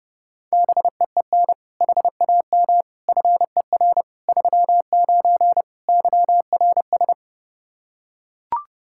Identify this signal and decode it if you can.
Morse code, heard as a keyed tone, about 32 wpm, 700 Hz. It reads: BEEN HAM FER 39 YRS
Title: Practice QSO - Exchange 3 - More Info - 30wpm